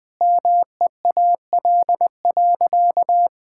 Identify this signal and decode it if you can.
Morse code, heard as a keyed tone, about 19 wpm, 700 Hz. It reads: MEAL.
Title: Difficult Sentences with Hints - 20wpm